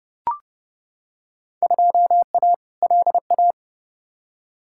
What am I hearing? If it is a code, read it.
Morse code, heard as a keyed tone, about 30 wpm, 700 Hz. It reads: 2A LA
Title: Field Day Exchanges Encore - 30wpm